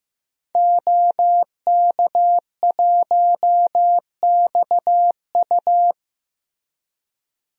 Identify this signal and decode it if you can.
Morse code, heard as a keyed tone, about 15 wpm, 700 Hz. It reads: OK1XU
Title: DXCC-100 Call Signs - 5 Characters - Repeated 2x - 15wpm